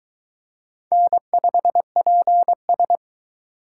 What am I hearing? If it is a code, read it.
Morse code, heard as a keyed tone, about 22 wpm, 700 Hz. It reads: N5PS